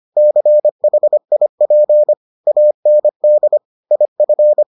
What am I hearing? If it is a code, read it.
Morse code, heard as a keyed tone, about 25 wpm, 600 Hz. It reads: CHIP AND IF